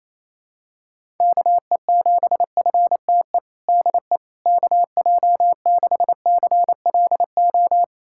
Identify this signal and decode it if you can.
Morse code, heard as a keyed tone, about 28 wpm, 700 Hz. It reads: KE7FTE DE KJ6CLO
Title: USA Callsign DE Callsign - 6 Characters - 28wpm